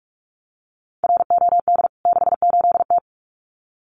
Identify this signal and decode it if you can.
Morse code, heard as a keyed tone, about 44 wpm, 700 Hz. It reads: ROD 68T
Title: CWT Exchange 45wpm